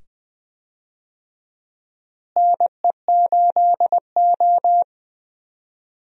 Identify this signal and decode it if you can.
Morse code, heard as a keyed tone, about 20 wpm, 700 Hz. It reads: NE8O
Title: USA Call Signs - 4 Characters - Repeated 2x - 20wpm